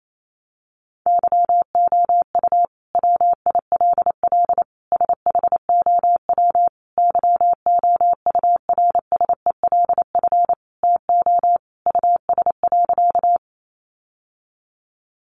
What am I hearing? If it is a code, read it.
Morse code, heard as a keyed tone, about 28 wpm, 700 Hz. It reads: YOU WILL SHOW YOURSELF TO US.
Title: Sentences from Top 600 Words 28wpm